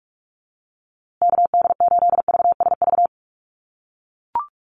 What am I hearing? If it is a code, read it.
Morse code, heard as a keyed tone, about 44 wpm, 700 Hz. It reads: KD8VSV
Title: USA Call Signs Repeated 3x 45wpm